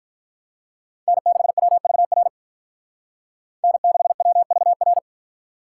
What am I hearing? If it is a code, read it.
Morse code, heard as a keyed tone, about 44 wpm, 700 Hz. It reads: N6WVR N6WVR
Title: Speed Racing - USA Call Signs 35wpm